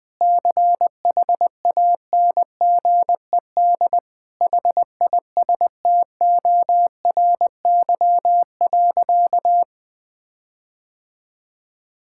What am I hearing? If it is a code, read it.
Morse code, heard as a keyed tone, about 20 wpm, 700 Hz. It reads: CHANGED HISTORY.
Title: Sentences with 7-Word Phrases - 20wpm